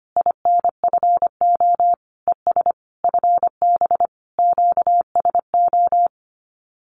Morse code, heard as a keyed tone, about 25 wpm, 700 Hz. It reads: INFO ES FB QSO
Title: Common QSO Phrases Encore 25wpm